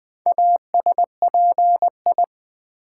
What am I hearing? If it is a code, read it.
Morse code, heard as a keyed tone, about 20 wpm, 700 Hz. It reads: ASPI
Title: Build A Word - 4 Letters At A Time - 10-Letter Words - 20wpm